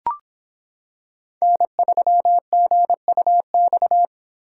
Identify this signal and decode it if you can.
Morse code, heard as a keyed tone, about 26 wpm, 700 Hz. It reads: N3GUX